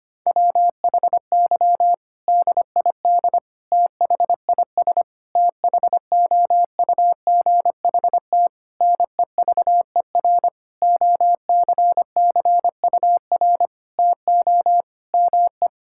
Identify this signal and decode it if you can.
Morse code, heard as a keyed tone, about 25 wpm, 700 Hz. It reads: WHY DID THIS THOUGHT NEVER OCCUR TO ME
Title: Sentences from Top 1000 Words - Review Part 6 - 25wpm